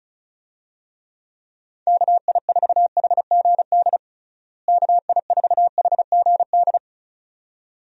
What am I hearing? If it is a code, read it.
Morse code, heard as a keyed tone, about 34 wpm, 700 Hz. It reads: KI4HGD KI4HGD
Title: USA Call Signs - 6 Characters - Repeated 2x - 35wpm